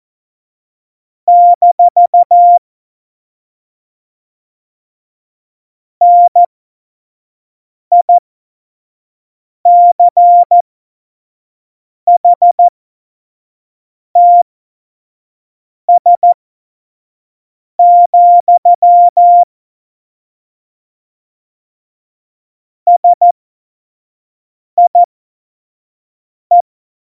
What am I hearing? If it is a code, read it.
Morse code, heard as a keyed tone, about 14 wpm, spaced as for 5 wpm, 700 Hz. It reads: - NICHTS, SIE
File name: Morsekurs / Witze / Witz 26. November 2019, 25 F70